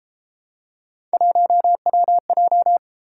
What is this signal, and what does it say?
Morse code, heard as a keyed tone, about 31 wpm, 700 Hz. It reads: 1WJ